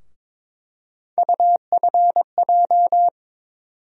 Morse code, heard as a keyed tone, about 22 wpm, 700 Hz. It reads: UFJ
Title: USA Call Sign Postfixes - 3 Letters 22wpm